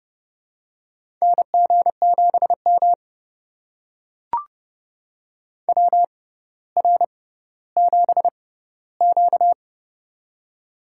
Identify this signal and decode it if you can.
Morse code, heard as a keyed tone, about 30 wpm, spaced as for 15 wpm, 700 Hz. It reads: NG7M WR7Q
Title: USA Call Signs - 4 Characters - 6x Character Spacing - 30wpm